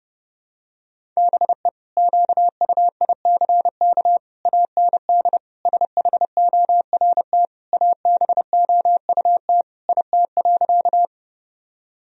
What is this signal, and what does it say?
Morse code, heard as a keyed tone, about 32 wpm, 700 Hz. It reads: BE QUICK AND SHORT ABOUT IT.
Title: Sentences from Top 1000 Words - Review Part 2 - 30wpm